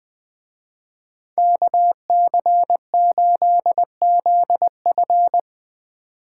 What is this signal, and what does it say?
Morse code, heard as a keyed tone, about 20 wpm, 700 Hz. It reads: KC8ZF
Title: Speed Racing - USA Call Signs 17wpm